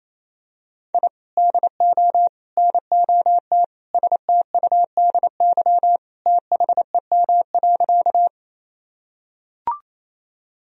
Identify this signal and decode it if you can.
Morse code, heard as a keyed tone, about 28 wpm, 700 Hz. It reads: I DO NOT STUDY THEM.
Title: Sentences from Top 1000 Words - Review Part 1 - 28wpm